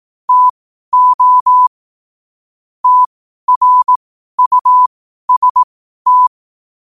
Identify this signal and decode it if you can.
Morse code, heard as a keyed tone, about 18 wpm, spaced as for 12 wpm, 1000 Hz. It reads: TO TRUST
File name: qotdfiles / 12wpm